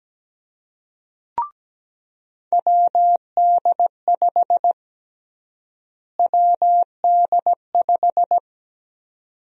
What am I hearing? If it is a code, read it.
Morse code, heard as a keyed tone, about 17 wpm, 700 Hz. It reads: WD5 WD5
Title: USA Call Sign Prefixes - Repeated 2x - 17wpm